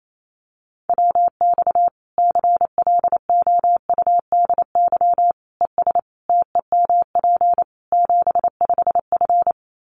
Morse code, heard as a keyed tone, about 28 wpm, 700 Hz. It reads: WX CLOUDY ES TEMP 75F
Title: Common QSO Phrases 28wpm